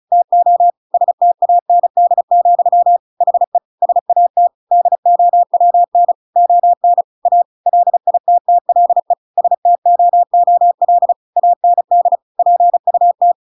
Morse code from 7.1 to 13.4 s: A LITTLE STOOL AND PUT